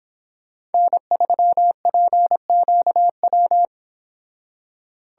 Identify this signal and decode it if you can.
Morse code, heard as a keyed tone, about 26 wpm, 700 Hz. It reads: N3PQW
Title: Speed Racing - USA 1x3 Call Signs 22wpm